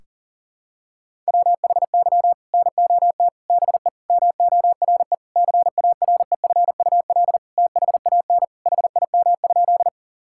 Morse code, heard as a keyed tone, about 40 wpm, 700 Hz. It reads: WHY NOT BE MORE CAREFUL THAN HIM?
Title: Sentences from Top 1600 Words 40wpm